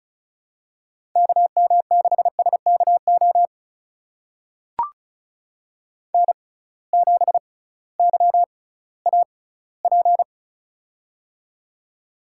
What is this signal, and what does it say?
Morse code, heard as a keyed tone, about 36 wpm, spaced as for 19 wpm, 700 Hz. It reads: KM6SKO N7YAP
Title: USA Call Signs - 6x Character Spacing - 35wpm